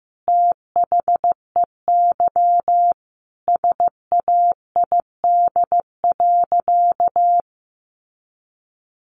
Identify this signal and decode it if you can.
Morse code, heard as a keyed tone, about 14 wpm, 700 Hz. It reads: THEY SAID.
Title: Sentences from Top 100 Words 15wpm